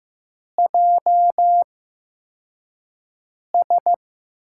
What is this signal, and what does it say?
Morse code, heard as a keyed tone, about 15 wpm, spaced as for 4 wpm, 700 Hz. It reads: JS
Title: USA Call Signs - 8x Character Spacing - 15wpm